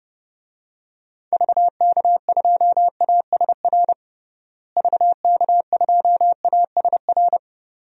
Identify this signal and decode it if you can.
Morse code, heard as a keyed tone, about 32 wpm, 700 Hz. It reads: VK2ASR VK2ASR
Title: DXCC-100 Call Signs - 6 Characters - Repeated 2x - 30wpm